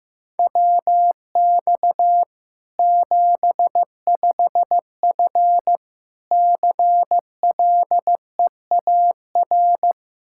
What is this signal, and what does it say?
Morse code, heard as a keyed tone, about 15 wpm, 700 Hz. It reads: WX 75F CLEAR
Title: Practice QSO - Exchange 2 - Info - 15wpm